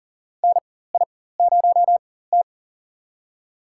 Morse code, heard as a keyed tone, about 40 wpm, spaced as for 19 wpm, 700 Hz. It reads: NI0T
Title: USA Call Signs - 4x Character Spacing - 40wpm